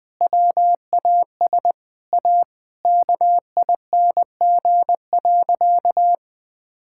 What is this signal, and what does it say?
Morse code, heard as a keyed tone, about 21 wpm, 700 Hz. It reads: WAS A KING.
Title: Sentences with 7-Word Phrases - 20wpm